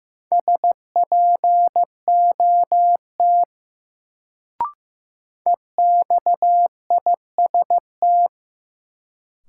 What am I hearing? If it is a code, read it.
Morse code, heard as a keyed tone, about 15 wpm, 700 Hz. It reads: SPOT EXIST
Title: Sentences from Top 1100 Words - New Word Drill Alt - 15wpm